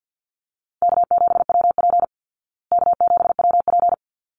Morse code, heard as a keyed tone, about 49 wpm, 700 Hz. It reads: K7WP K7WP
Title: Easy Call Signs Repeated 2x - 50wpm